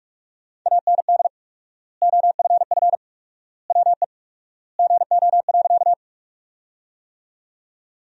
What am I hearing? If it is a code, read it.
Morse code, heard as a keyed tone, about 47 wpm, 700 Hz. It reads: AND OFF WE GO.
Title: 4x Word-Spacing - Sentences from Top 200 Words - 45wpm